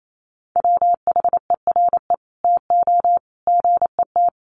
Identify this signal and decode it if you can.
Morse code, heard as a keyed tone, about 28 wpm, 700 Hz. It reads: WHERE TO GET
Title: Sentences from Top 1400 Words 28wpm